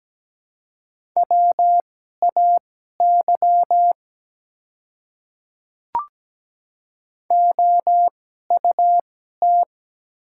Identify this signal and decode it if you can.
Morse code, heard as a keyed tone, about 17 wpm, spaced as for 12 wpm, 700 Hz. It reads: WAY OUT
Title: 2x Character Spacing - Top 100 Words - 17wpm